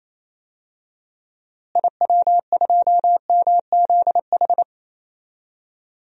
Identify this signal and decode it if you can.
Morse code, heard as a keyed tone, about 27 wpm, 700 Hz. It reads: IW2MZH
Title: DXCC-100 Call Signs - 6 Characters - Repeated 3x - 28wpm